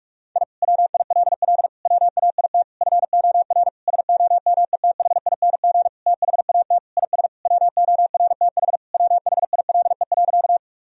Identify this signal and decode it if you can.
Morse code, heard as a keyed tone, about 46 wpm, 700 Hz. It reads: I WILL WAIT FOR SOMETHING THAT IS WORTH WHILE.
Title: Sentences from Top 1000 Words - Review Part 1 - 45wpm